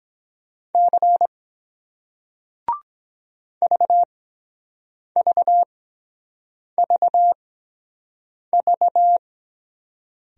Morse code, heard as a keyed tone, about 23 wpm, spaced as for 7 wpm, 700 Hz. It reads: C VVVV